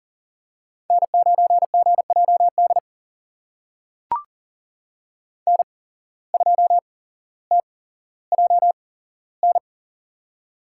Morse code, heard as a keyed tone, about 40 wpm, spaced as for 17 wpm, 700 Hz. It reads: N9GJD N2TJN
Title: USA Call Signs - 5 Characters - 8x Character Spacing - 40wpm